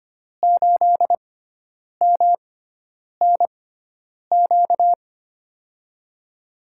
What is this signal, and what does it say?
Morse code, heard as a keyed tone, about 25 wpm, spaced as for 9 wpm, 700 Hz. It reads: 8MNQ
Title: USA Call Signs - 6x Character Spacing - 25wpm